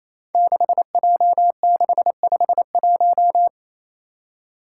Morse code, heard as a keyed tone, about 28 wpm, 700 Hz. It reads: 6J651
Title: Five Challenging Letter-Number Combinations 28wpm